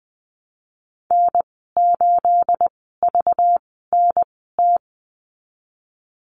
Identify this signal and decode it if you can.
Morse code, heard as a keyed tone, about 20 wpm, spaced as for 15 wpm, 700 Hz. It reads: N8VNT
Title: 231 Lesson 37g - Letter x - Call Signs 20wpm